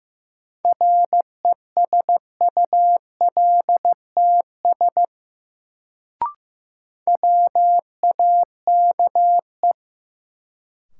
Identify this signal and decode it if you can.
Morse code, heard as a keyed tone, about 15 wpm, 700 Hz. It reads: RESULTS WAKE
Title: Sets of 4 Words - New Word Drill - 15wpm